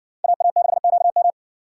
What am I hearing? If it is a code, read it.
Morse code, heard as a keyed tone, about 60 wpm, 700 Hz. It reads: AA6QM